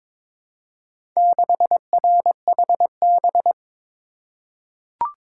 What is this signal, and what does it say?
Morse code, heard as a keyed tone, about 22 wpm, 690 Hz. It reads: 6RHB
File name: practice / 22wpm / Four Letter-Number Combinations - 22wpm